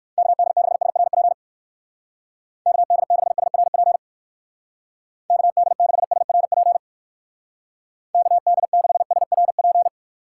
Morse code, heard as a keyed tone, about 58 wpm, 700 Hz. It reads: KD6SRP KD6SRP KD6SRP KD6SRP